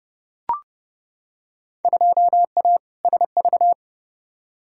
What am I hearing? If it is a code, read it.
Morse code, heard as a keyed tone, about 30 wpm, 700 Hz. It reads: 2A SV
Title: Field Day Exchanges - 30wpm